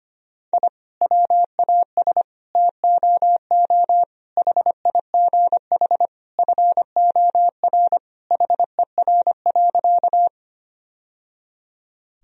Morse code, heard as a keyed tone, about 25 wpm, 700 Hz. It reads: I WAS TOO HIGH FOR HER.
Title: Sentences from Top 1000 Words - Review Part 1 - 25wpm